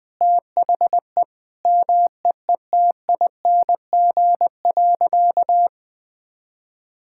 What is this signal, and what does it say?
Morse code, heard as a keyed tone, about 20 wpm, 700 Hz. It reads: THE MEETING.
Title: Sentences with 5-Word Phrases - 20wpm